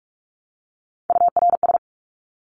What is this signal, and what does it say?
Morse code, heard as a keyed tone, about 44 wpm, 700 Hz. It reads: URS
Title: Abbreviations Repeated 3x - 45wpm